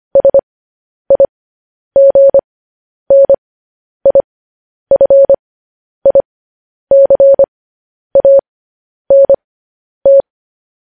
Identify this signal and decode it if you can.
Morse code, heard as a keyed tone, about 25 wpm, spaced as for 10 wpm, 550 Hz. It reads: SIGNIFICANT
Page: News Headlines in Morse Code at 10 WPM